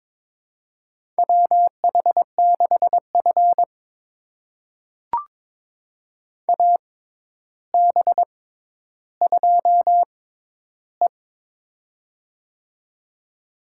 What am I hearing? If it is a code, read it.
Morse code, heard as a keyed tone, about 21 wpm, spaced as for 11 wpm, 700 Hz. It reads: WH6F AB2E
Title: USA Call Signs - 4 Characters - 6x Character Spacing - 22wpm